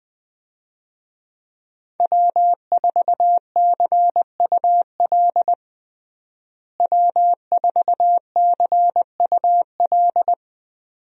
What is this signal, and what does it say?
Morse code, heard as a keyed tone, about 20 wpm, 700 Hz. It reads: W4CUL W4CUL
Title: USA Call Signs - 5 Characters - Repeated 2x - 20wpm